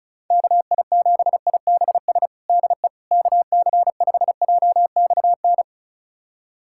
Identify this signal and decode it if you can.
Morse code, heard as a keyed tone, about 36 wpm, 700 Hz. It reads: KI7IBS DE KC5JXN
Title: USA Callsign DE Callsign 35wpm